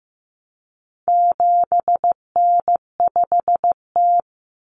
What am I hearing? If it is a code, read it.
Morse code, heard as a keyed tone, about 15 wpm, 700 Hz. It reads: 7N5T
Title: Common Cut Numbers - 4-Digit 15wpm